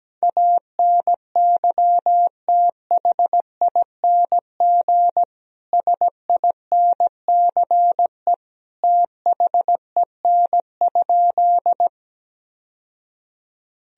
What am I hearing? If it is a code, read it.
Morse code, heard as a keyed tone, about 17 wpm, 700 Hz. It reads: ANYTHING SINCE THEN?
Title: Sentences from Top 1000 Words - Review Part 1 - 17wpm